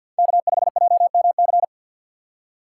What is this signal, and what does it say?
Morse code, heard as a keyed tone, about 50 wpm, 700 Hz. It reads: K5JMC
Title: USA Call Signs - 5 Characters - 2x Character Spacing - 50wpm